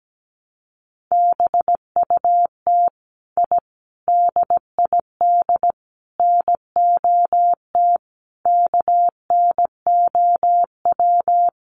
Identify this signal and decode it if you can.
Morse code, heard as a keyed tone, about 17 wpm, 700 Hz. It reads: BUT I DID NOT KNOW
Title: Sets of 5 Words Encore 17wpm